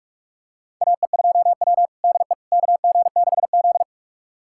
Morse code, heard as a keyed tone, about 46 wpm, 700 Hz. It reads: AE2W DE KG6Z
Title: USA Callsign DE Callsign - 4 Characters - 45wpm